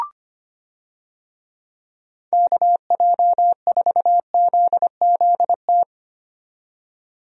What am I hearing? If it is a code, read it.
Morse code, heard as a keyed tone, about 25 wpm, 700 Hz. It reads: KJ4ZZT
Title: USA Call Signs - 6 Characters - 25wpm